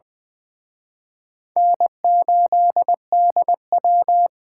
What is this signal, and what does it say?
Morse code, heard as a keyed tone, about 25 wpm, 700 Hz. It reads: N8DW
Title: Speed Racing - USA Call Signs 20wpm